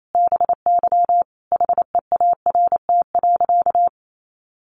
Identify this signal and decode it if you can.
Morse code, heard as a keyed tone, about 28 wpm, 700 Hz. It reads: BY HEART.
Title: Sentences from Top 1200 Words 28wpm